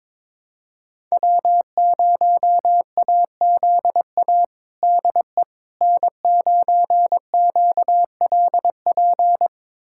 Morse code, heard as a keyed tone, about 21 wpm, 700 Hz. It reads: W0AZA DE N9QLP
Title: USA Callsign DE Callsign - 5 Characters - 22wpm